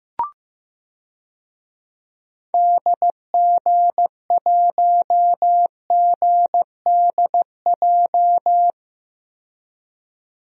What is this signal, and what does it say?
Morse code, heard as a keyed tone, about 15 wpm, 700 Hz. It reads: DG1GDJ
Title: DXCC-100 Call Signs - 6 Characters - 15wpm